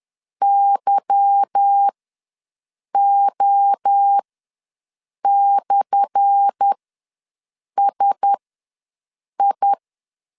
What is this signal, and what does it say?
Morse code, heard as a keyed tone, about 10 wpm, spaced as for 6 wpm, 800 Hz. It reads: YO/SI
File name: LEZIONI / 10 SIGNS / 10A SLASH ws 10wpm 4121